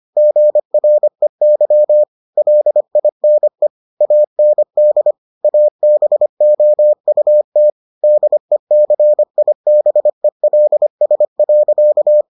Morse code, heard as a keyed tone, about 25 wpm, 600 Hz. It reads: GREY LINE AND ABOUT DECIBELS.